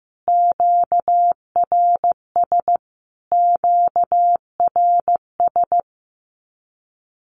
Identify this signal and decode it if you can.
Morse code, heard as a keyed tone, about 14 wpm, 700 Hz. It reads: QRS QRS
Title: Q-Codes Repeated 3x 15wpm